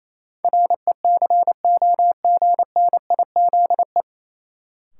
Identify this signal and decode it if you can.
Morse code, heard as a keyed tone, about 28 wpm, 700 Hz. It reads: RECOGNIZE
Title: Sentences from Top 1400 Words - New Word Drill Alt - 28wpm